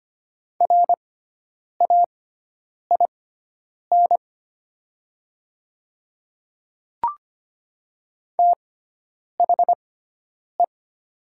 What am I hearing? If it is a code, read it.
Morse code, heard as a keyed tone, about 25 wpm, spaced as for 9 wpm, 700 Hz. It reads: RAIN THE
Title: 6x Character Spacing - Top 300 Words - 25wpm